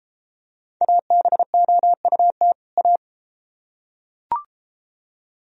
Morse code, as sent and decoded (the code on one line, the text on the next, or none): .- -... --- ..- - / .-
ABOUT A